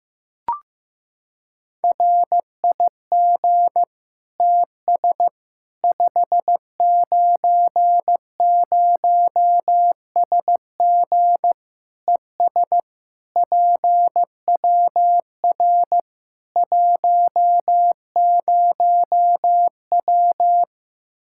Morse code, heard as a keyed tone, about 16 wpm, 700 Hz. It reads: RIG TS 590SG ES PWR 10W
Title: Practice QSO - Exchange 2 - Info - 15wpm